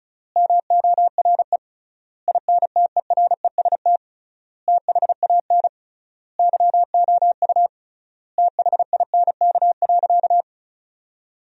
Morse code, heard as a keyed tone, about 34 wpm, 700 Hz. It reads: MORE INTEREST THAN YOU THINK.
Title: 3x Word-Spacing - Sentences from Top 400 Words - 35wpm